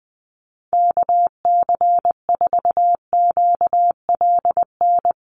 Morse code, heard as a keyed tone, about 20 wpm, 700 Hz. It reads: KC4QLN
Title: USA Call Signs 20wpm